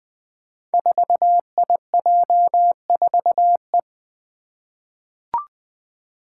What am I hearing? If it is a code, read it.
Morse code, heard as a keyed tone, about 20 wpm, 700 Hz. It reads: 4IJ4E
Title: Five Challenging Letter-Number Combinations 20wpm